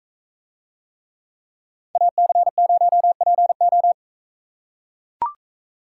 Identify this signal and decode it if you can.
Morse code, heard as a keyed tone, about 41 wpm, 700 Hz. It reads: AC0PO